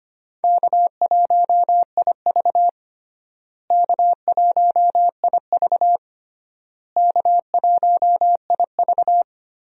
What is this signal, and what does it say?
Morse code, heard as a keyed tone, about 25 wpm, 700 Hz. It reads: K1IV K1IV K1IV
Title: USA Call Signs - 4 Characters - Repeated 3x - 25wpm